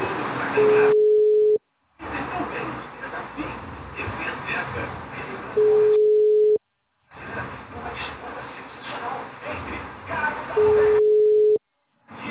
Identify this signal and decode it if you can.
Ringback tone (Europe, 425 Hz).
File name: out- -2008- - - .34